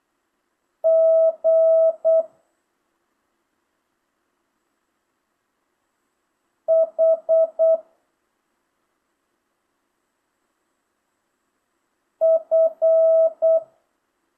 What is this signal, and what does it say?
Morse code, heard as a keyed tone, about 8 wpm, 650 Hz. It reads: G H F
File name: jh8cbh / 2025cw